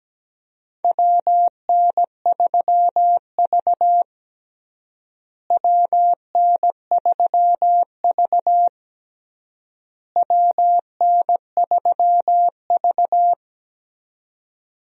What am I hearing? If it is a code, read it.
Morse code, heard as a keyed tone, about 17 wpm, 700 Hz. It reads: WN3V WN3V WN3V
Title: USA Call Signs - 4 Characters - Repeated 3x - 17wpm